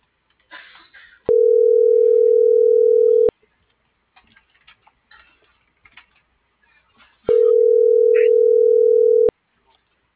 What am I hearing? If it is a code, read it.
Ringback tone (North America), 440 + 480 Hz.